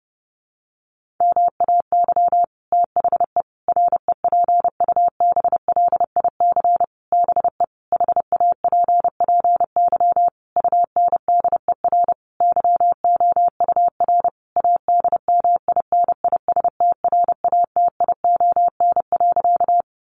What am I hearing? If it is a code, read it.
Morse code, heard as a keyed tone, about 30 wpm, 700 Hz. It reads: MAY THE REPUBLIC BE HAPPY UNDER YOUR ADMINISTRATION.
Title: Sentences from Top 1500 Words 30wpm